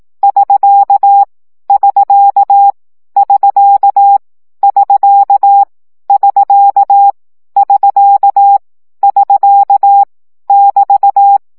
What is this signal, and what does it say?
Morse code, heard as a keyed tone, about 18 wpm, 800 Hz. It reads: <SK> <SK> <SK> <SK> <SK> <SK> <SK> <BT>
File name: CW gyakorlas sk 15 baud